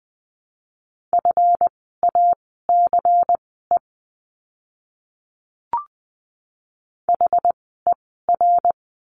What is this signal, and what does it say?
Morse code, heard as a keyed tone, about 20 wpm, spaced as for 14 wpm, 700 Hz. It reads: FACE HER
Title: 139 Lesson 23f - Letter f - Words 20wpm